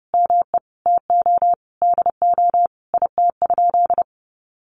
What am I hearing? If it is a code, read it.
Morse code, heard as a keyed tone, about 30 wpm, 700 Hz. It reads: ME TO DO IT?
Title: Sentences from Top 200 Words 30wpm